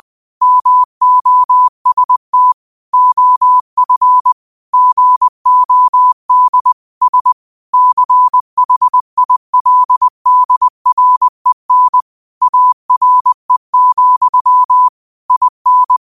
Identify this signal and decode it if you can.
Morse code, heard as a keyed tone, about 20 wpm, 1000 Hz. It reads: MOST OF GOD S CHILDREN ARE, IN
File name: qotdfiles / 20wpm